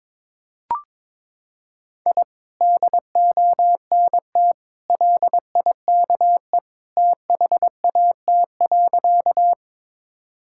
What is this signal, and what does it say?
Morse code, heard as a keyed tone, about 22 wpm, 700 Hz. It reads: I DONT LIKE THAT.
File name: practice / 22wpm / Sentences with 2-Word Phrases - 22wpm